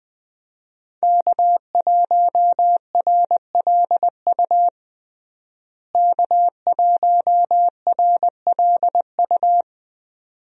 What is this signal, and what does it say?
Morse code, heard as a keyed tone, about 21 wpm, 700 Hz. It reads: K1RLU K1RLU
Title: USA Call Signs - 5 Characters - Repeated 2x - 20wpm